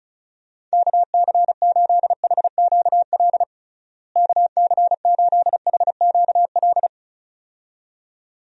Morse code, heard as a keyed tone, about 36 wpm, 700 Hz. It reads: KC8HQL KC8HQL
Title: USA Call Signs - 6 Characters - Repeated 2x - 35wpm